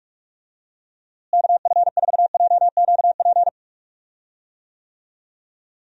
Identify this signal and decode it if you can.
Morse code, heard as a keyed tone, about 46 wpm, 700 Hz. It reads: KF4JQP
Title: USA Call Signs - 6 Characters - Repeated 2x - 45wpm